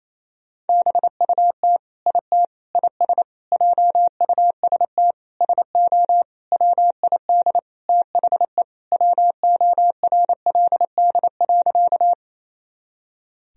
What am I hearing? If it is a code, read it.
Morse code, heard as a keyed tone, about 28 wpm, 700 Hz. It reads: BUT IT IS JUST SO WID THE WORLD.
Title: Sentences from Top 200 Words and Abbreviations 28wpm